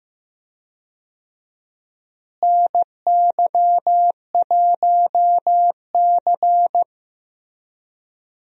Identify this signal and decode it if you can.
Morse code, heard as a keyed tone, about 16 wpm, 700 Hz. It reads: NY1C
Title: USA Call Signs - 4 Characters - Repeated 2x - 15wpm